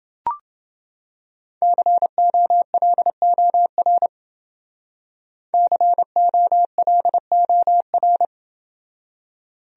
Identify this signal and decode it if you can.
Morse code, heard as a keyed tone, about 28 wpm, 700 Hz. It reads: COLOR COLOR